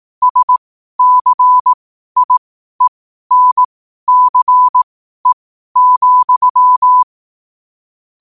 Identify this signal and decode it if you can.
Morse code, heard as a keyed tone, about 18 wpm, spaced as for 12 wpm, 1000 Hz. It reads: SCIENCE,
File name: 12wpm